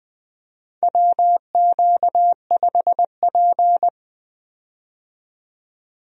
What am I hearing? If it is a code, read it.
Morse code, heard as a keyed tone, about 20 wpm, 700 Hz. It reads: WQ5P
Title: USA Call Signs - 4 Characters - 20wpm